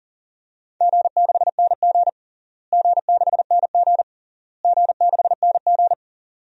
Morse code, heard as a keyed tone, about 37 wpm, 700 Hz. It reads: G6NG G6NG G6NG
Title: DXCC-100 Call Signs - 4 Characters - Repeated 3x - 40wpm